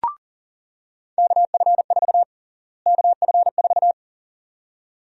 Morse code, heard as a keyed tone, about 40 wpm, 700 Hz. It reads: KF4 KF4
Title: USA Call Sign Prefixes - Repeated 2x - 40wpm